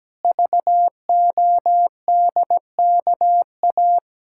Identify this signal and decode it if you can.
Morse code, heard as a keyed tone, about 17 wpm, 700 Hz. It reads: VODKA
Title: Pangram Challenge v2 - 17wpm